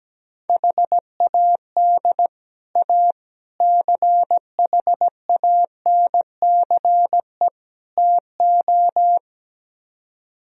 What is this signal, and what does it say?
Morse code, heard as a keyed tone, about 17 wpm, 700 Hz. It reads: HAD A CHANCE TO
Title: Speed-Racing - Sets of 4 Words 17wpm